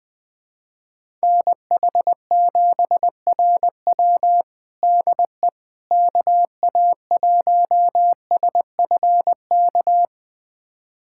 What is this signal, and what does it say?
Morse code, heard as a keyed tone, about 21 wpm, 700 Hz. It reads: NH7RW DE KA1SFK
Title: USA Callsign DE Callsign 20wpm